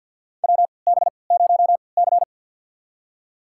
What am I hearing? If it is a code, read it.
Morse code, heard as a keyed tone, about 51 wpm, spaced as for 28 wpm, 700 Hz. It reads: WB0C
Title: USA Call Signs - 4 Characters - 3x Character Spacing - 50wpm